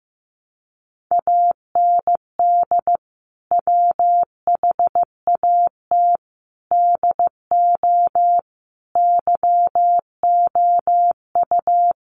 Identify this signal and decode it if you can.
Morse code, heard as a keyed tone, about 15 wpm, 700 Hz. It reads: AND WHAT DO YOU
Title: Sentences from Top 200 Words 15wpm